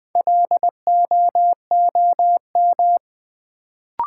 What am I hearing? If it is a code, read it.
Morse code, heard as a keyed tone, about 20 wpm, 700 Hz. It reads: LOOM
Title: Build A Word - 2 Letters At A Time - 4-Letter Words - 20wpm